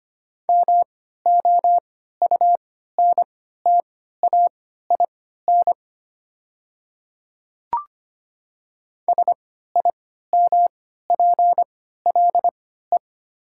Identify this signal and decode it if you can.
Morse code, heard as a keyed tone, about 25 wpm, spaced as for 14 wpm, 700 Hz. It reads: MOUNTAIN SIMPLE
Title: 3x Character Spacing - Top 500 Words - 25wpm